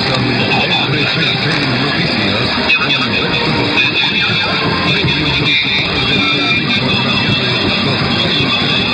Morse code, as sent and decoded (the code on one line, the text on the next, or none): none